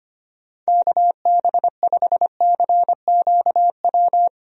Morse code, heard as a keyed tone, about 25 wpm, 700 Hz. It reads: KB5CQW
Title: USA Call Signs - 6 Characters - Repeated 2x - 25wpm